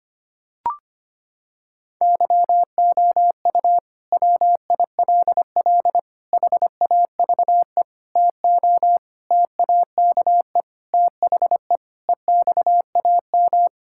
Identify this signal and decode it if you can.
Morse code, heard as a keyed tone, about 25 wpm, 700 Hz. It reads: YOU WILL HAVE TO TAKE THE EXAM
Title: Sentences from Top 1000 Words - Review Part 5 - 25wpm